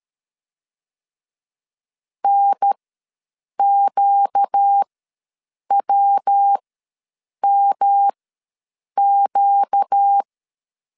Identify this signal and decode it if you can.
Morse code, heard as a keyed tone, about 13 wpm, spaced as for 7 wpm, 800 Hz. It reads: NQWMQ
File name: 5 LB ws 12wpm